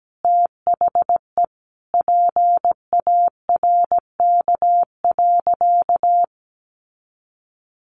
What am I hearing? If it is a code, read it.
Morse code, heard as a keyed tone, about 17 wpm, 700 Hz. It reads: THE PARK.
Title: Sentences from Top 1300 Words 17wpm